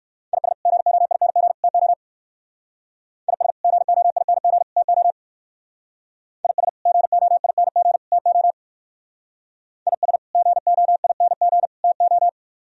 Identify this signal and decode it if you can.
Morse code, heard as a keyed tone, about 58 wpm, 700 Hz. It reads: IS GOING TO IS GOING TO IS GOING TO IS GOING TO